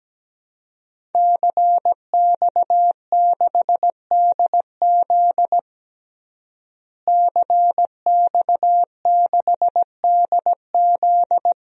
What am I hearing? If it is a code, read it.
Morse code, heard as a keyed tone, about 17 wpm, 700 Hz. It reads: CX6DZ CX6DZ
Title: DXCC-100 Call Signs - 5 Characters - Repeated 3x - 17wpm